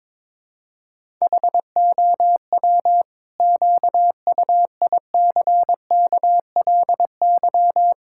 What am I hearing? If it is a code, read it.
Morse code, heard as a keyed tone, about 22 wpm, 700 Hz. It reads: HOW QUICKLY
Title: Sentences from Top 1000 Words - Review Part 5 - 22wpm